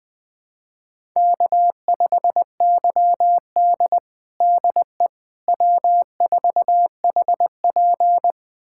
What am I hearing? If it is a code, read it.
Morse code, heard as a keyed tone, about 21 wpm, 700 Hz. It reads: K5YD DE W4HP
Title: USA Callsign DE Callsign - 4 Characters - 20wpm